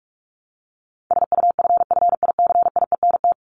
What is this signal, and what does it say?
Morse code, heard as a keyed tone, about 44 wpm, 700 Hz. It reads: SUFFICIENT
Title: LY Suffix - Top 100 Words with Base Words 45wpm